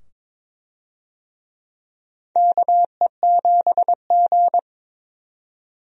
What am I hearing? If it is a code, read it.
Morse code, heard as a keyed tone, about 22 wpm, 700 Hz. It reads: KE7G
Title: USA Call Signs - 4 Characters - 22wpm